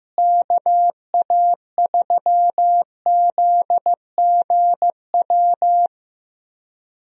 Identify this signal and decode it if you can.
Morse code, heard as a keyed tone, about 15 wpm, 700 Hz. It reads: KA3ZGW
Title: USA Call Signs - 6 Characters - Repeated 3x - 15wpm